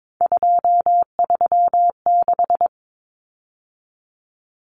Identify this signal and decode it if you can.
Morse code, heard as a keyed tone, about 22 wpm, 700 Hz. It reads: 236
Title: Numbers - 3-Digit 22wpm